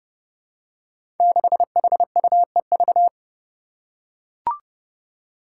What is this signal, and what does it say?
Morse code, heard as a keyed tone, about 30 wpm, 700 Hz. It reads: 6HUEV
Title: Five Challenging Letter-Number Combinations 30wpm